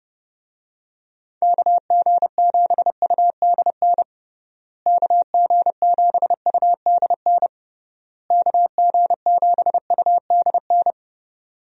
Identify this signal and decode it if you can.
Morse code, heard as a keyed tone, about 29 wpm, 700 Hz. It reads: KG7UDN KG7UDN KG7UDN
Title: USA Call Signs - 6 Characters - Repeated 3x - 30wpm